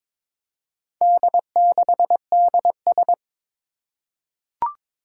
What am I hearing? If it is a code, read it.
Morse code, heard as a keyed tone, about 22 wpm, 700 Hz. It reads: D6DS
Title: Four Challenging Letter-Number Combinations 22wpm